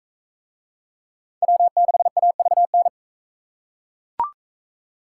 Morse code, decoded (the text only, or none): W6AVN